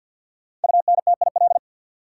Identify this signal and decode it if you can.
Morse code, heard as a keyed tone, about 49 wpm, 700 Hz. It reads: UNTIL
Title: Five Letter Words - 50wpm